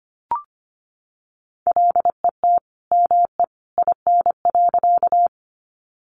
Morse code, heard as a keyed tone, about 25 wpm, 700 Hz. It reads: LET ME IN.
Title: Sentences from Top 300 Words 25wpm